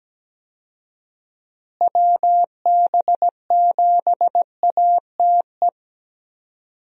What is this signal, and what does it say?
Morse code, heard as a keyed tone, about 17 wpm, 700 Hz. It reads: WB7ATE
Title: USA Call Signs - 6 Characters - Repeated 2x - 17wpm